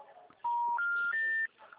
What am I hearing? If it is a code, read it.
Special information tone.
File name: out- -2018- - - .5095